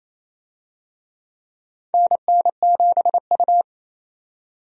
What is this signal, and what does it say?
Morse code, heard as a keyed tone, about 28 wpm, 700 Hz. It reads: NN7U
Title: USA Call Signs - 4 Characters - 28wpm